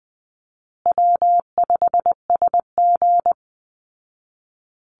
Morse code, heard as a keyed tone, about 20 wpm, 700 Hz. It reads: W5SG
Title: Call Signs Repeated 2x - 20wpm